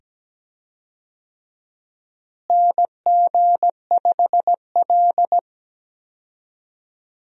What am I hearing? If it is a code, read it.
Morse code, heard as a keyed tone, about 17 wpm, 700 Hz. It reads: NG5L
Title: USA Call Signs - 4 Characters - Repeated 2x - 17wpm